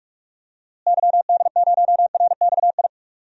Morse code, heard as a keyed tone, about 43 wpm, 700 Hz. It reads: YD0RXI
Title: DXCC-100 Call Signs - 6 Characters - Repeated 2x - 45wpm